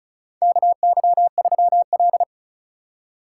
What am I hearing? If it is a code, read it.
Morse code, heard as a keyed tone, about 34 wpm, 700 Hz. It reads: KY3L
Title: USA Call Signs - 4 Characters - 6x Character Spacing - 35wpm